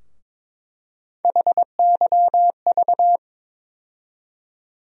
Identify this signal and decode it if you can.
Morse code, heard as a keyed tone, about 22 wpm, 700 Hz. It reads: HYV
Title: USA Call Sign Postfixes - 3 Letters 22wpm